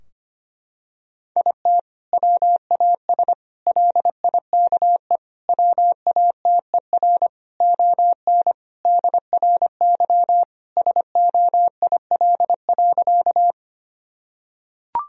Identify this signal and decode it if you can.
Morse code, heard as a keyed tone, about 25 wpm, 700 Hz. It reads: IT WAS LIKE WATER ON DRY SOIL.
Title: Sentences from Top 1000 Words - Review Part 4 - 25wpm